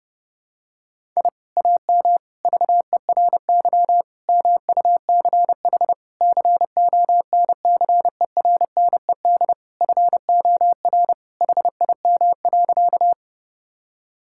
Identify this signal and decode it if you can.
Morse code, heard as a keyed tone, about 30 wpm, 700 Hz. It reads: I AM VERY MUCH CONCERNED FOR HIM.
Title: Sentences from Top 1000 Words - Review Part 6 - 30wpm